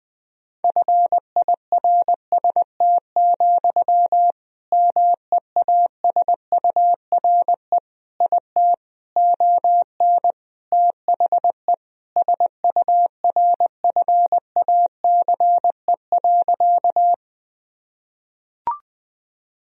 Morse code, heard as a keyed tone, about 20 wpm, 700 Hz. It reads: FIRST, MEASURE IT ON THE SURFACE.
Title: Sentences from Top 1000 Words - Review Part 3 - 20wpm